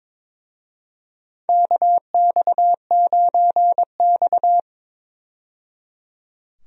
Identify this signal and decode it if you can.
Morse code, heard as a keyed tone, about 22 wpm, 700 Hz. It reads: KX9X
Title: USA Call Signs - 4 Characters - 22wpm